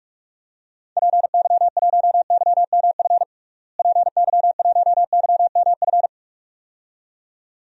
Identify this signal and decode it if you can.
Morse code, heard as a keyed tone, about 44 wpm, 700 Hz. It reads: PY1YMF PY1YMF
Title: DXCC-100 Call Signs - 6 Characters - Repeated 2x - 45wpm